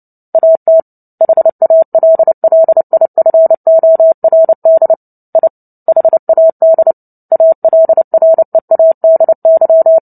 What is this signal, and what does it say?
Morse code, heard as a keyed tone, about 30 wpm, 650 Hz. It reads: AT HALLIFORD I HAD ALREADY
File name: Book Two / 03-Chapter 3 - 29 WPM